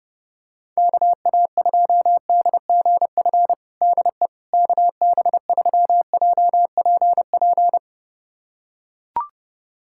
Morse code, heard as a keyed tone, about 32 wpm, 700 Hz. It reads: KA2DGF DE KB3JPP
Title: USA Callsign DE Callsign - 6 Characters - 30wpm